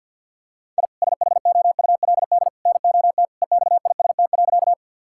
Morse code, heard as a keyed tone, about 50 wpm, 700 Hz. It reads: I SHOULD NOT EXIST.
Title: Sentences from Top 1000 Words - Review Part 5 - 50wpm